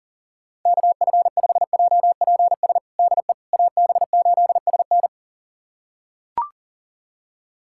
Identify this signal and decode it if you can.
Morse code, heard as a keyed tone, about 37 wpm, 700 Hz. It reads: KF5JPS DE AB8SN
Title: USA Callsign DE Callsign 40wpm